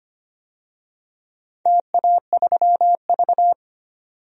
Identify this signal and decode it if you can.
Morse code, heard as a keyed tone, about 25 wpm, 700 Hz. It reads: TA3V
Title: DXCC-100 Call Signs - 4 Characters - 25wpm